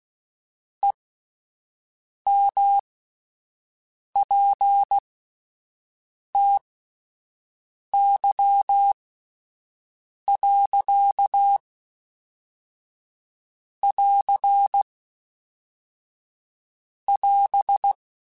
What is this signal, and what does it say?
Morse code, heard as a keyed tone, about 16 wpm, spaced as for 5 wpm, 800 Hz. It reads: EMPTY.<AR><AS>
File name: Spaced / Chuckles Spaced-mp3s / HalfFull sp @10